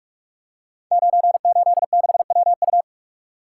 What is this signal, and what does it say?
Morse code, heard as a keyed tone, about 44 wpm, 700 Hz. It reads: 986WU
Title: Code Groups - 5 Letter:Numbers - 45wpm